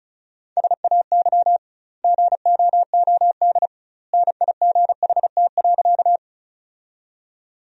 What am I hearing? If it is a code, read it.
Morse code, heard as a keyed tone, about 36 wpm, 700 Hz. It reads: SAY GOOD NIGHT.
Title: 2x Word-Spacing - Sentences from Top 300 Words - 35wpm